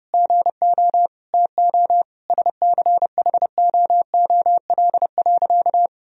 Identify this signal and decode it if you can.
Morse code, heard as a keyed tone, about 30 wpm, 700 Hz. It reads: GO TO SCHOOL.
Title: Sentences from Top 1000 Words - Review Part 2 - 30wpm